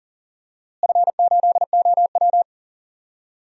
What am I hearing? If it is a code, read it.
Morse code, heard as a keyed tone, about 40 wpm, 700 Hz. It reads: F8OW
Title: DXCC-100 Call Signs - 4 Characters - Repeated 3x - 40wpm